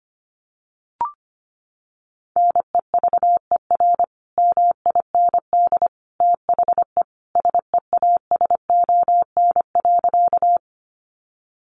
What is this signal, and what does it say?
Morse code, heard as a keyed tone, about 25 wpm, 700 Hz. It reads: NEVER MIND THE SEASON.
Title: Sentences from Top 1400 Words 25wpm